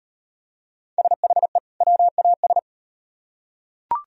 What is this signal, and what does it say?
Morse code, heard as a keyed tone, about 38 wpm, 700 Hz. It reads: SHE WAS